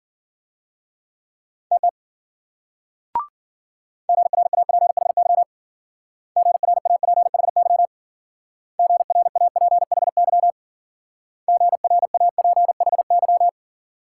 Morse code, heard as a keyed tone, about 53 wpm, 700 Hz. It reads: TT GRAPHY GRAPHY GRAPHY GRAPHY